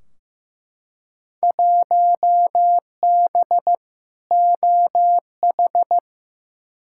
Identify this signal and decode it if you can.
Morse code, heard as a keyed tone, about 15 wpm, 700 Hz. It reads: 1B OH
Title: Field Day Exchanges - 15wpm